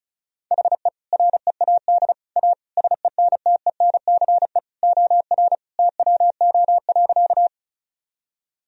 Morse code, heard as a keyed tone, about 36 wpm, 700 Hz. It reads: HE READ A SENTENCE OR TWO.
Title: Sentences from Top 1000 Words - Review Part 1 - 35wpm